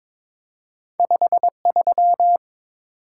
Morse code, heard as a keyed tone, about 22 wpm, 700 Hz. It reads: 53
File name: practice / 22wpm / Two Letter-Number Combinations - 22wpm